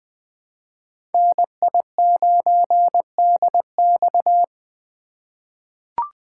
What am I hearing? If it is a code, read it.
Morse code, heard as a keyed tone, about 20 wpm, 700 Hz. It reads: NI9DX
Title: USA Call Signs - 3x Character Spacing - 20wpm